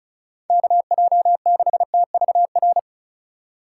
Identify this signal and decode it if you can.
Morse code, heard as a keyed tone, about 36 wpm, 690 Hz. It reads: KJ6TVR
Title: USA Call Signs - 6 Characters - 6x Character Spacing - 35wpm